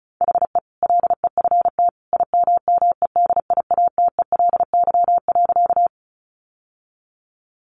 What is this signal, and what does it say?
Morse code, heard as a keyed tone, about 34 wpm, 700 Hz. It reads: HE LEFT IMMEDIATELY.
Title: Sentences from Top 1100 Words 35wpm